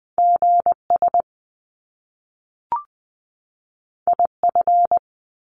Morse code, heard as a keyed tone, about 20 wpm, 700 Hz. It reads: GS IF
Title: Top Two Letter Combinations in Words 20wpm